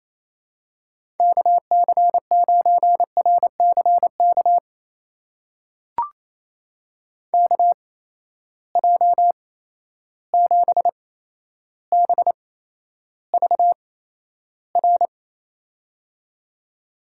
Morse code, heard as a keyed tone, about 28 wpm, spaced as for 12 wpm, 700 Hz. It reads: KC9RCK KJ7BVR
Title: USA Call Signs - 6 Characters - 8x Character Spacing - 28wpm